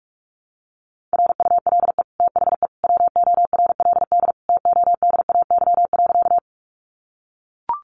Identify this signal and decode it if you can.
Morse code, heard as a keyed tone, about 44 wpm, 700 Hz. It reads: RULE THE WORLD TODAY.
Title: Sentences from Top 600 Words 45wpm